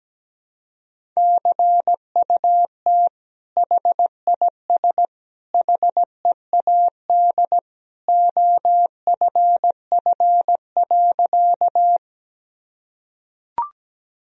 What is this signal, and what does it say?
Morse code, heard as a keyed tone, about 17 wpm, 700 Hz. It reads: CUT HIS HEAD OFF.
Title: Sentences from Top 1000 Words - Review Part 1 - 17wpm